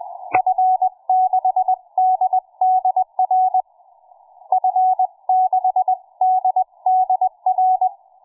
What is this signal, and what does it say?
Morse code, heard as a keyed tone, about 20 wpm, 740 Hz. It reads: F6DDR F6DDR
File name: cw forum / fm5cd 1833 le17avril08 06h51